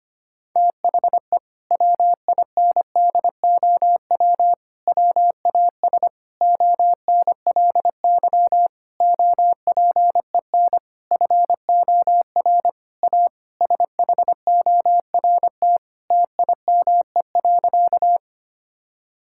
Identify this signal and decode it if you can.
Morse code, heard as a keyed tone, about 25 wpm, 700 Hz. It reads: THE WINDOW WAS ONLY OPEN FOR A SHORT TIME.
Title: Sentences from Top 1600 Words 25wpm